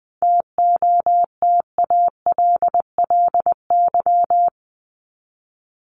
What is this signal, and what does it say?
Morse code, heard as a keyed tone, about 20 wpm, 700 Hz. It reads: TOTALLY
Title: LY Suffix - Top 100 Words with Base Words 20wpm